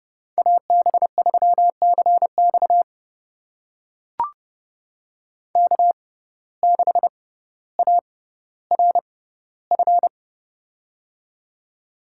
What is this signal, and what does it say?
Morse code, heard as a keyed tone, about 32 wpm, spaced as for 16 wpm, 700 Hz. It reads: AB3CX K6ARF
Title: USA Call Signs - 5 Characters - 6x Character Spacing - 30wpm